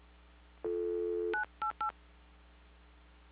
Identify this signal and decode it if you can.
DTMF, heard as telephone touch tones, tones of about 110 ms. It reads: 988